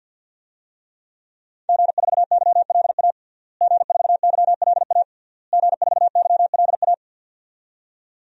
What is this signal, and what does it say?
Morse code, heard as a keyed tone, about 52 wpm, 700 Hz. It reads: G4YLA G4YLA G4YLA
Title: DXCC-100 Call Signs - 5 Characters - Repeated 3x - 50wpm